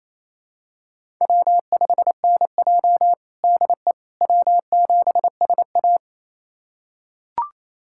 Morse code, heard as a keyed tone, about 30 wpm, 700 Hz. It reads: W5NJ DE W7SA
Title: USA Callsign DE Callsign - 4 Characters - 28wpm